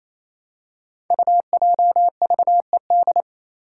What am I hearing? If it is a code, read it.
Morse code, heard as a keyed tone, about 29 wpm, 700 Hz. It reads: UJVED
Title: Five Challenging Letter-Number Combinations 28wpm